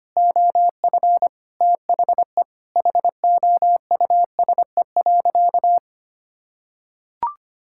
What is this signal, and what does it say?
Morse code, heard as a keyed tone, about 25 wpm, 700 Hz. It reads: OF THE HOUSE.
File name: practice / 25wpm / Sentences with 4-Word Phrases - 25wpm